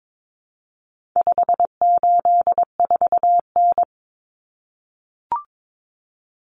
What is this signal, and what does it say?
Morse code, heard as a keyed tone, about 22 wpm, 700 Hz. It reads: 584N
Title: Common Cut Numbers - 4-Digit 22wpm